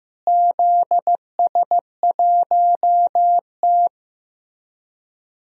Morse code, heard as a keyed tone, about 16 wpm, 700 Hz. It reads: ZS1T
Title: DXCC-100 Call Signs - 4 Characters - Repeated 3x - 15wpm